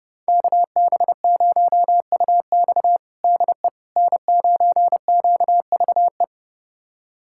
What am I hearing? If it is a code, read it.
Morse code, heard as a keyed tone, about 30 wpm, 700 Hz. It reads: KB0UX DE N9QVE
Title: USA Callsign DE Callsign 30wpm